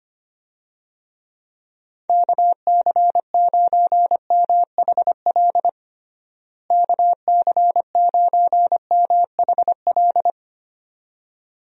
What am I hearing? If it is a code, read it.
Morse code, heard as a keyed tone, about 25 wpm, 700 Hz. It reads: KC9MHL KC9MHL
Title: USA Call Signs - 6 Characters - Repeated 2x - 25wpm